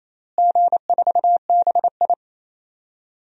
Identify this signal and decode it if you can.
Morse code, heard as a keyed tone, about 28 wpm, 700 Hz. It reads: G4BI
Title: DXCC-100 Call Signs - 4 Characters - Repeated 2x - 28wpm